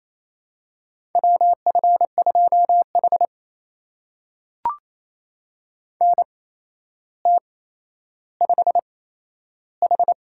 Morse code, heard as a keyed tone, about 28 wpm, spaced as for 12 wpm, 700 Hz. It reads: WF2H NT5H
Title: USA Call Signs - 4 Characters - 8x Character Spacing - 28wpm